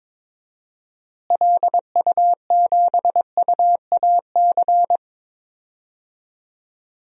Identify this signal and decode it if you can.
Morse code, heard as a keyed tone, about 22 wpm, 700 Hz. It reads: LU7UAC
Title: DXCC-100 Call Signs - 6 Characters - Repeated 3x - 22wpm